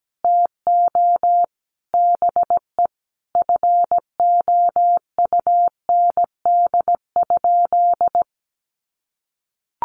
Morse code, heard as a keyed tone, about 17 wpm, 700 Hz. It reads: TO BE FOUND?
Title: Sentences from Top 300 Words 17wpm